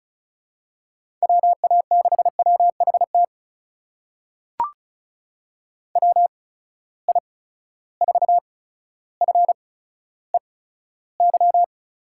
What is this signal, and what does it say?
Morse code, heard as a keyed tone, about 36 wpm, spaced as for 15 wpm, 700 Hz. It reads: WA6WHT WI4FEY
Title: USA Call Signs - 6 Characters - 8x Character Spacing - 35wpm